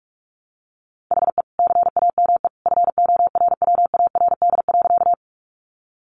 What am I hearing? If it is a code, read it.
Morse code, heard as a keyed tone, about 44 wpm, 700 Hz. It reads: HE CAME FORWARD.
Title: Sentences from Top 1300 Words 45wpm